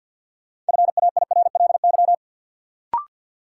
Morse code, heard as a keyed tone, about 48 wpm, 700 Hz. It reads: FAIRLY
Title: Sentences from Top 1100 Words - New Word Drill Alt - 50wpm